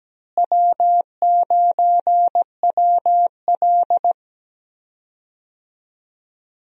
Morse code, heard as a keyed tone, about 17 wpm, 700 Hz. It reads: W9WL
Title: USA Call Signs - 4 Characters - Repeated 3x - 17wpm